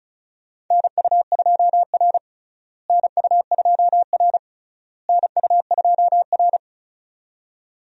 Morse code, heard as a keyed tone, about 36 wpm, 700 Hz. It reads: NU2R NU2R NU2R
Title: USA Call Signs - 4 Characters - Repeated 3x - 35wpm